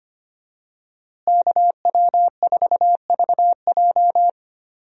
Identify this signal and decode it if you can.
Morse code, heard as a keyed tone, about 25 wpm, 700 Hz. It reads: KW4VJ
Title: USA Call Signs - 5 Characters - Repeated 3x - 25wpm